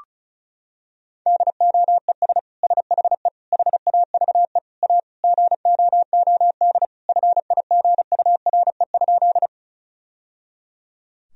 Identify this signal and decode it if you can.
Morse code, heard as a keyed tone, about 34 wpm, 700 Hz. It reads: DOES SHE HAVE A GOOD FIGURE?
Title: Sentences from Top 1000 Words - Review Part 2 - 35wpm